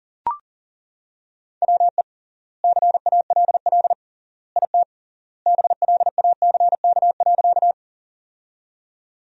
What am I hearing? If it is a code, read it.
Morse code, heard as a keyed tone, about 40 wpm, 700 Hz. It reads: WE CALL IT BLACK.
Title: 3x Word-Spacing - Sentences from Top 400 Words - 40wpm